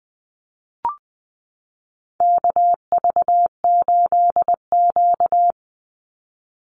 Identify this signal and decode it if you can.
Morse code, heard as a keyed tone, about 20 wpm, 700 Hz. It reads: KV8Q
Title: Call Signs - 20wpm